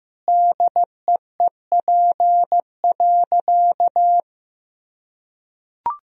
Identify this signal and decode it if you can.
Morse code, heard as a keyed tone, about 15 wpm, 700 Hz. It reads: DEEP.
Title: Sentences from Top 1000 Words - Review Part 3 - 15wpm